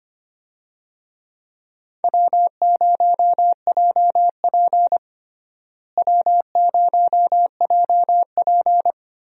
Morse code, heard as a keyed tone, about 25 wpm, 700 Hz. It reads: W0JP W0JP
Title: USA Call Signs - 4 Characters - Repeated 2x - 25wpm